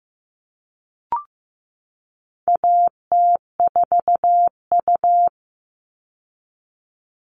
Morse code, heard as a keyed tone, about 15 wpm, 700 Hz. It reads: AT4U
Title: All Cut Numbers - 4-Digit 15wpm